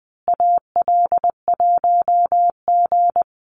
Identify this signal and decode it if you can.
Morse code, heard as a keyed tone, about 20 wpm, 700 Hz. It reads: AL1G
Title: Call Signs Repeated 2x - 20wpm